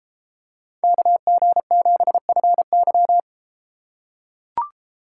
Morse code, heard as a keyed tone, about 33 wpm, 700 Hz. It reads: KG7FY